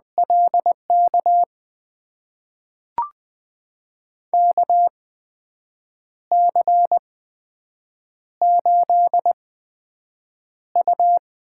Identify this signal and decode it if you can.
Morse code, heard as a keyed tone, about 20 wpm, spaced as for 7 wpm, 700 Hz. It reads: LK KC8U
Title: USA Call Signs - 6 Characters - 8x Character Spacing - 20wpm